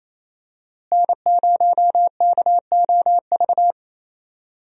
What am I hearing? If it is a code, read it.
Morse code, heard as a keyed tone, about 28 wpm, 700 Hz. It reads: N0KOV
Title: USA Call Signs - 5 Characters - Repeated 3x - 28wpm